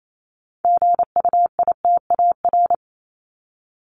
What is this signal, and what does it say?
Morse code, heard as a keyed tone, about 28 wpm, 700 Hz. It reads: GUITAR
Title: Top 1501-1750 Words 28wpm